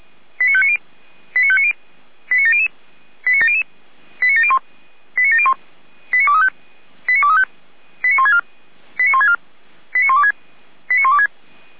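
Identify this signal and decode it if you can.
Five-tone selective call (ZVEI), heard as quick runs of five tones, tones of about 70 ms. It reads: 89590 89590 89700 89700 89701 89701 89235 89235 89164 89164 89126 89126
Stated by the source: ZVEI five-tone selective call received on an SDR.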